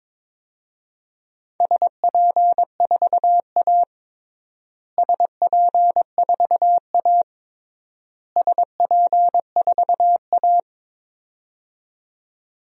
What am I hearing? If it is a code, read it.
Morse code, heard as a keyed tone, about 22 wpm, 700 Hz. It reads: SP4A SP4A SP4A
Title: DXCC-100 Call Signs - 4 Characters - Repeated 3x - 22wpm